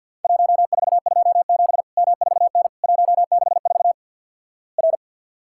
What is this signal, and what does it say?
Morse code, heard as a keyed tone, about 49 wpm, 700 Hz. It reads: 1427 M4N 164 R
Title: Worked All Europe DX Contest - QTC Exchange - 50wpm